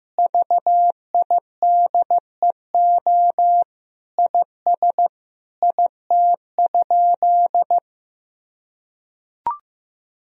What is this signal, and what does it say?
Morse code, heard as a keyed tone, about 15 wpm, 700 Hz. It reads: VIDEO IS IT?
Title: Sentences from Top 1000 Words - Review Part 4 - 15wpm